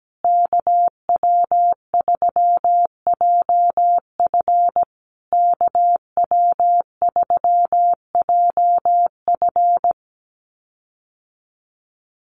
Morse code, heard as a keyed tone, about 17 wpm, 700 Hz. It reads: KW3JF KW3JF
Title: USA Call Signs Repeated 3x 17wpm